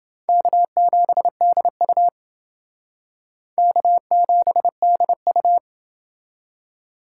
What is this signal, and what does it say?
Morse code, heard as a keyed tone, about 28 wpm, 700 Hz. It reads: K7DU K7DU